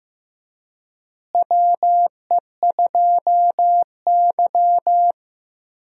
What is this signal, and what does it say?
Morse code, heard as a keyed tone, about 15 wpm, 700 Hz. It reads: WE2Y
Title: USA Call Signs - 4 Characters - Repeated 2x - 15wpm